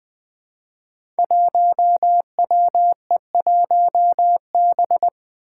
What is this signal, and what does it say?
Morse code, heard as a keyed tone, about 20 wpm, 700 Hz. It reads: 1WE1B
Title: Five Challenging Letter-Number Combinations 20wpm